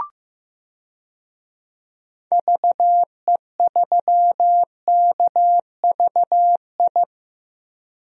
Morse code, heard as a keyed tone, about 15 wpm, 700 Hz. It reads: VE3KVI
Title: DXCC-100 Call Signs - 6 Characters - 15wpm